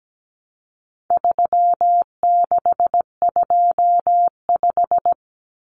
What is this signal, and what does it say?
Morse code, heard as a keyed tone, about 17 wpm, 700 Hz. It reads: 3625
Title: Common Cut Numbers - 4-Digit 17wpm